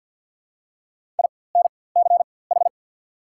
Morse code, heard as a keyed tone, about 49 wpm, spaced as for 23 wpm, 690 Hz. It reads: INCH
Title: 4x Character Spacing - Top 500 Words - 50wpm